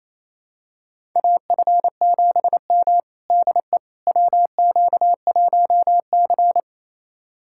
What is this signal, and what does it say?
Morse code, heard as a keyed tone, about 28 wpm, 700 Hz. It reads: AF7M DE WQ1C
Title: USA Callsign DE Callsign - 4 Characters - 28wpm